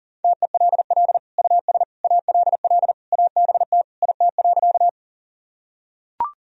Morse code, heard as a keyed tone, about 37 wpm, 700 Hz. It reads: TELL US ALL ABT IT.
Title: Sentences from Top 200 Words and Abbreviations 40wpm